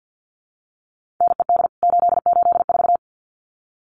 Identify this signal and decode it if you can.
Morse code, heard as a keyed tone, about 50 wpm, 700 Hz. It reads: NED 884
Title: CWT Exchange 50wpm